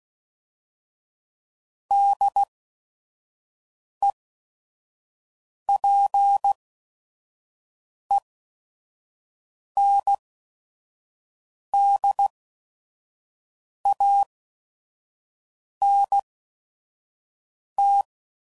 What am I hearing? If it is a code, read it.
Morse code, heard as a keyed tone, about 16 wpm, spaced as for 5 wpm, 800 Hz. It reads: DEPENDANT
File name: CW Code Groups @5WPM / Technical Subjects / XC